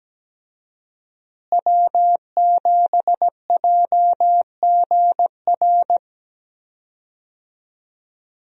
Morse code, heard as a keyed tone, about 17 wpm, 700 Hz. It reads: W7JGR
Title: USA Call Signs - 5 Characters - Repeated 3x - 17wpm